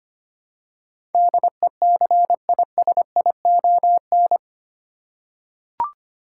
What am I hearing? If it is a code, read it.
Morse code, heard as a keyed tone, about 25 wpm, 700 Hz. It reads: DECISION
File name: Eight Letter Words - 25wpm